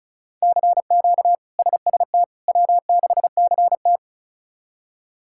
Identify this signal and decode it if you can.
Morse code, heard as a keyed tone, about 36 wpm, 700 Hz. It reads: CQ SST W6CT
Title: K1USN SST QSO Practice - 35wpm